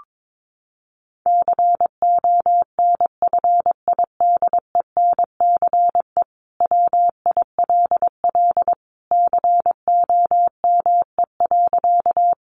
Morse code, heard as a keyed tone, about 22 wpm, 700 Hz. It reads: CONFIDENCE WILL COME.
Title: Sentences from Top 1500 Words 22wpm